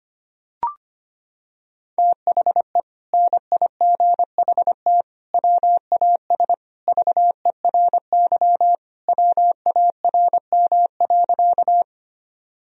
Morse code, heard as a keyed tone, about 25 wpm, 700 Hz. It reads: THE NIGHT WAS VERY WARM.
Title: Sentences from Top 1000 Words - Review Part 2 - 25wpm